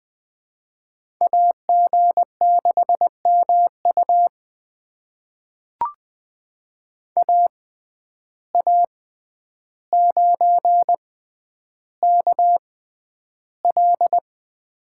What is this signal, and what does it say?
Morse code, heard as a keyed tone, about 20 wpm, spaced as for 10 wpm, 700 Hz. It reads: AG6MU AA9KL
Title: USA Call Signs - 5 Characters - 6x Character Spacing - 20wpm